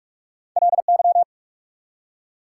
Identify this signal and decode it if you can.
Morse code, heard as a keyed tone, about 47 wpm, 690 Hz. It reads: LY
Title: Prefixes and Suffixes 45wpm